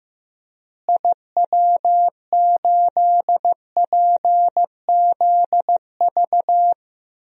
Morse code, heard as a keyed tone, about 15 wpm, 700 Hz. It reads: IW8PZV
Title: DXCC-100 Call Signs - 6 Characters - Repeated 3x - 15wpm